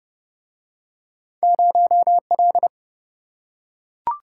Morse code, heard as a keyed tone, about 28 wpm, 700 Hz. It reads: 0L